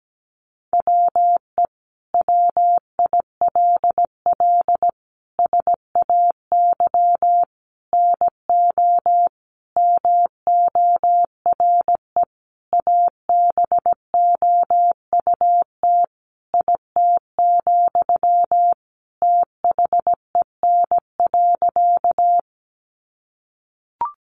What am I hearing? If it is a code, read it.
Morse code, heard as a keyed tone, about 17 wpm, 700 Hz. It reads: WE WILL SAY NO MORE ABOUT IT, THEN.
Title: Sentences from Top 200 Words 17wpm